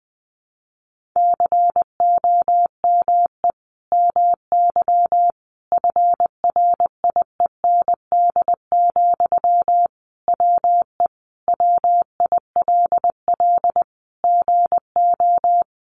Morse code, heard as a keyed tone, about 20 wpm, 700 Hz. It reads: COME MY FRIEND, WE WILL GO
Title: Sentences from Top 1300 Words 20wpm